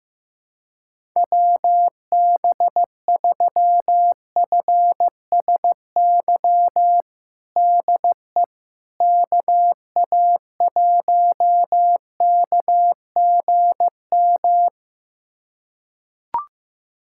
Morse code, heard as a keyed tone, about 15 wpm, 700 Hz. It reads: WB3FSY DE KA1KGM
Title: USA Callsign DE Callsign - 6 Characters - 15wpm